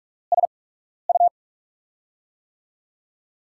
Morse code, heard as a keyed tone, about 46 wpm, spaced as for 12 wpm, 700 Hz. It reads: SU
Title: USA Call Signs - 8x Character Spacing - 45wpm